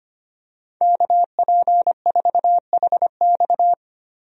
Morse code, heard as a keyed tone, about 25 wpm, 700 Hz. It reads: KP4HX
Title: USA Call Signs - 5 Characters - Repeated 3x - 25wpm